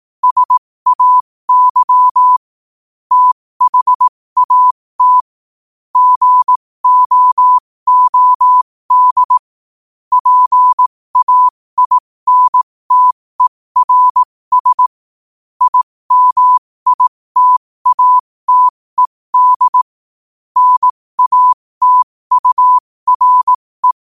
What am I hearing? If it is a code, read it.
Morse code, heard as a keyed tone, about 18 wpm, spaced as for 15 wpm, 1000 Hz. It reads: SAY THAT GOOD PAINTERS IMITATED NATURE
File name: 15wpm